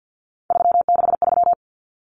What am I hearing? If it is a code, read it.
Morse code, heard as a keyed tone, about 50 wpm, 700 Hz. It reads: 363
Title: Numbers - 3-Digit 50wpm